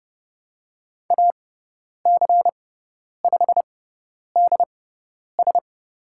Morse code, heard as a keyed tone, about 30 wpm, spaced as for 10 wpm, 700 Hz. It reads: AC5DS
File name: Lesson 17g - Letter u - Call Signs 30wpm